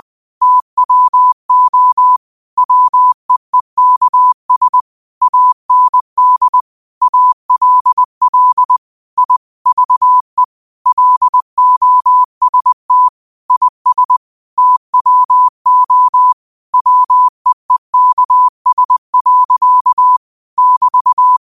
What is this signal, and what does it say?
Morse code, heard as a keyed tone, about 20 wpm, 1000 Hz. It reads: TWO WEEKS AND ALL I VE LOST IS TWO WEEKS. <BT>
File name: qotdfiles / 20wpm